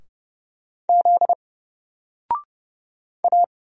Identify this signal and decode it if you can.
Morse code, heard as a keyed tone, about 30 wpm, 700 Hz. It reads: Z A
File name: Single-Letter-Number-30wpm